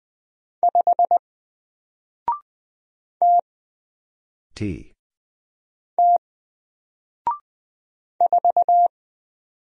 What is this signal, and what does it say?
Morse code, heard as a keyed tone, about 20 wpm, 700 Hz. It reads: 5 T T 4
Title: Single Letter-Number Alt 20wpm